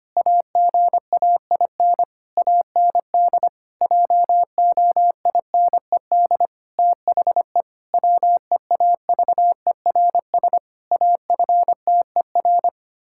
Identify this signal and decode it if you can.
Morse code, heard as a keyed tone, about 25 wpm, 700 Hz. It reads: AGAIN AND JOINED THE WEAVERS AFTER
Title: Pangram Challenge v2 - 25wpm